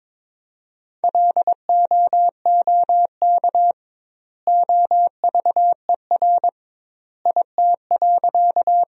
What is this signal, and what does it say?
Morse code, heard as a keyed tone, about 21 wpm, 700 Hz. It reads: LOOK OVER IT.
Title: 2x Word-Spacing - Sentences from Top 100 Words - 22wpm